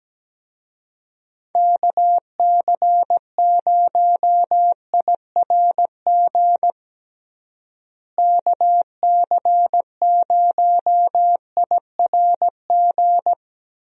Morse code, heard as a keyed tone, about 17 wpm, 700 Hz. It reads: KC0IRG KC0IRG
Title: USA Call Signs - 6 Characters - Repeated 2x - 17wpm